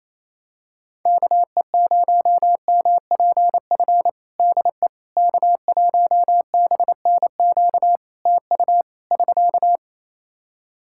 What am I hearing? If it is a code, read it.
Morse code, heard as a keyed tone, about 28 wpm, 700 Hz. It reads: KE0MPF DE K1BNQ TU <SK>
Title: Practice QSO - Exchange 4 - Ending - 28wpm